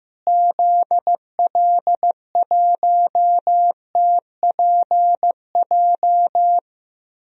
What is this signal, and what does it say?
Morse code, heard as a keyed tone, about 15 wpm, 700 Hz. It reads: ZL1TPJ
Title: DXCC-100 Call Signs - 6 Characters - Repeated 3x - 15wpm